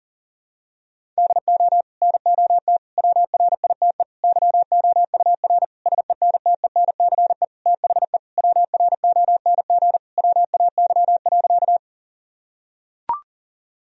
Morse code, heard as a keyed tone, about 40 wpm, 700 Hz. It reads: DO NOT WRITE YOUR SENTENCE THE WRONG WAY.
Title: Sentences from Top 1000 Words - Review Part 1 - 40wpm